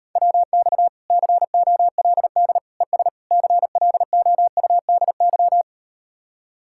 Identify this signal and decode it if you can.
Morse code, heard as a keyed tone, about 39 wpm, 700 Hz. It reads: WX COLD ES CLOUDY